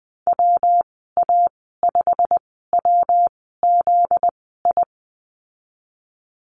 Morse code, heard as a keyed tone, about 20 wpm, spaced as for 15 wpm, 700 Hz. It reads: WA5WZI
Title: 245 Lesson 39g - Letter z - Call Signs 20wpm